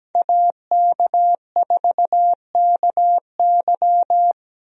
Morse code, heard as a keyed tone, about 17 wpm, 700 Hz. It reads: AK4KY
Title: USA Call Signs - 5 Characters - Repeated 3x - 17wpm